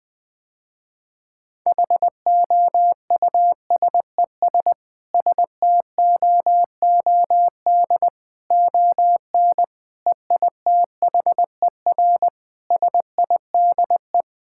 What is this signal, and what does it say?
Morse code, heard as a keyed tone, about 20 wpm, 700 Hz. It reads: HOUSES STOOD ON EITHER SIDE
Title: Sentences with 5-Word Phrases - Spoken Beginning - 20wpm